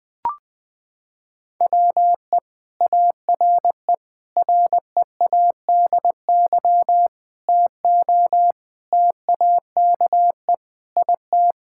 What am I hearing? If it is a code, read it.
Morse code, heard as a keyed tone, about 21 wpm, 700 Hz. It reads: WE ARE READY TO TAKE IT
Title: Sentences with 7-Word Phrases - 20wpm